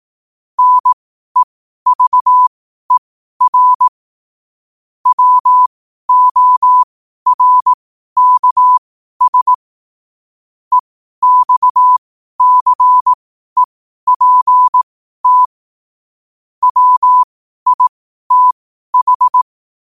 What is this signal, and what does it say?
Morse code, heard as a keyed tone, about 18 wpm, spaced as for 12 wpm, 1000 Hz. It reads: NEVER WORKS EXCEPT WITH
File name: qotdfiles / 12wpm